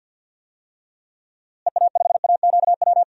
Morse code, heard as a keyed tone, about 48 wpm, 700 Hz. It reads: EA5AQW
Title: DXCC-100 Call Signs - 6 Characters - Repeated 3x - 50wpm